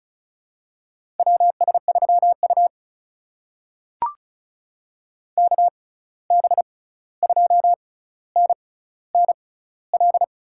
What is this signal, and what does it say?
Morse code, heard as a keyed tone, about 36 wpm, spaced as for 16 wpm, 700 Hz. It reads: WS3U KB2NNL
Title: USA Call Signs - 6x Character Spacing - 35wpm